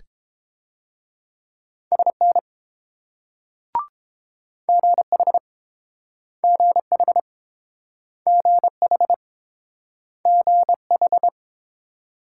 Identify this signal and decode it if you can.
Morse code, heard as a keyed tone, about 29 wpm, 700 Hz. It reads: SN GH GH GH GH